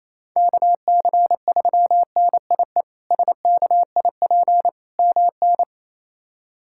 Morse code, heard as a keyed tone, about 28 wpm, 700 Hz. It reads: KC3NIE SKIP MN
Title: K1USN SST QSO Practice - 28wpm